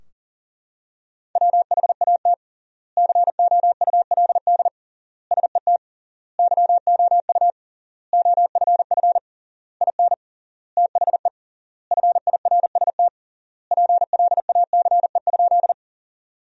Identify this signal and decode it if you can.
Morse code, heard as a keyed tone, about 37 wpm, 700 Hz. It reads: WHAT COULD SET YOU OFF IN THE FIRST PLACE?
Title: 3x Word-Spacing - Sentences from Top 200 Words - 40wpm